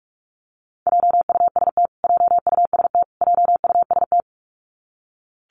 Morse code, heard as a keyed tone, about 44 wpm, 700 Hz. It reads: JUST JUST JUST
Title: Top 500 Words Repeated 45wpm